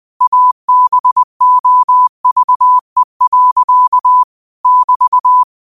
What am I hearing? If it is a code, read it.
Morse code, heard as a keyed tone, about 20 wpm, 1000 Hz. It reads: ABOVE. <BT>
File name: qotdfiles / 20wpm